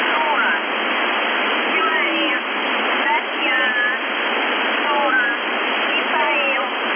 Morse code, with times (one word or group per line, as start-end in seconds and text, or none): none